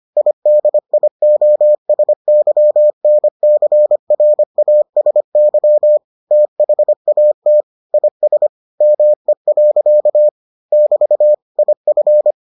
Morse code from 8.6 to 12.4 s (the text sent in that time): ME. <BT> IF